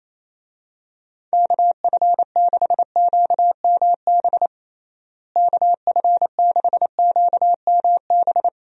Morse code, heard as a keyed tone, about 27 wpm, 700 Hz. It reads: KF6QMB KF6QMB
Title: USA Call Signs - 6 Characters - Repeated 3x - 28wpm